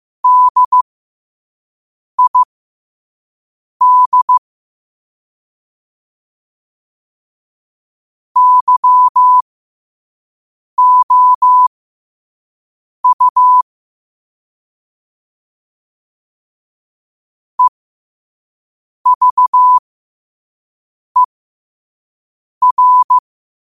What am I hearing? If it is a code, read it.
Morse code, heard as a keyed tone, about 15 wpm, spaced as for 5 wpm, 1000 Hz. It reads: DID YOU EVER
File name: qotdfiles / 5wpm